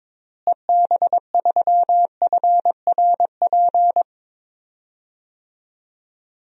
Morse code, heard as a keyed tone, about 22 wpm, 700 Hz. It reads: EB3FRP
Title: DXCC-100 Call Signs - 6 Characters - Repeated 2x - 22wpm